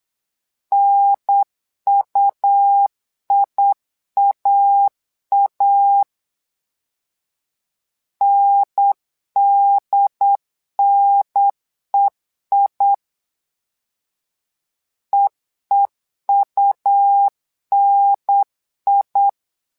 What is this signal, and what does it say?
Morse code, heard as a keyed tone, about 8 wpm, 800 Hz. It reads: NUIAA NDNEI EEUNI
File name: LEZIONI / 2 UHDN / 2 UHDN 8wpm2232